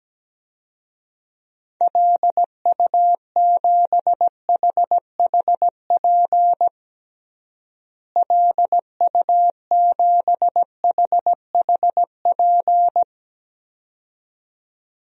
Morse code, heard as a keyed tone, about 17 wpm, 700 Hz. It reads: LU7HHP LU7HHP
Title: DXCC-100 Call Signs - 6 Characters - Repeated 2x - 17wpm